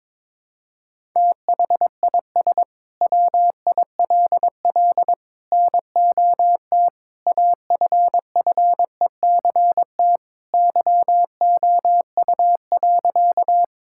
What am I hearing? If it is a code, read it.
Morse code, heard as a keyed tone, about 22 wpm, 700 Hz. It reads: THIS WILL NOT AFFECT YOU.
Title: Sentences from Top 1000 Words - Review Part 6 - 22wpm